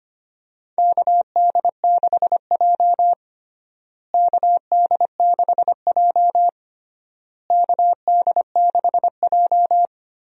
Morse code, heard as a keyed tone, about 25 wpm, 700 Hz. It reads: KD6J KD6J KD6J
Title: USA Call Signs - 4 Characters - Repeated 3x - 25wpm